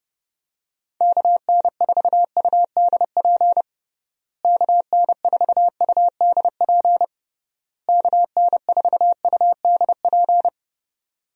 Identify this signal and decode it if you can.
Morse code, heard as a keyed tone, about 32 wpm, 700 Hz. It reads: KN4UDP KN4UDP KN4UDP
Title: USA Call Signs - 6 Characters - Repeated 3x - 30wpm